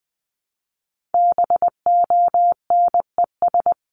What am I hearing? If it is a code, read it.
Morse code, heard as a keyed tone, about 20 wpm, 700 Hz. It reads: BONES
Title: A Sound Of Thunder 20wpm